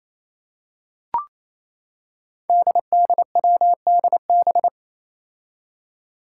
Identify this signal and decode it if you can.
Morse code, heard as a keyed tone, about 27 wpm, 700 Hz. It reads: DDWDB
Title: Five Challenging Letter-Number Combinations 28wpm